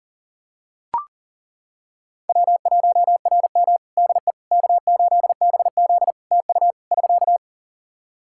Morse code, heard as a keyed tone, about 40 wpm, 700 Hz. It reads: W1RM DE K8BZ TU <SK>
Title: Practice QSO - Exchange 4 - Ending - 40wpm